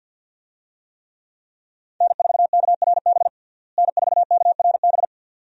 Morse code, heard as a keyed tone, about 51 wpm, 700 Hz. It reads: N4KRB N4KRB
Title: USA Call Signs - 5 Characters - Repeated 2x - 50wpm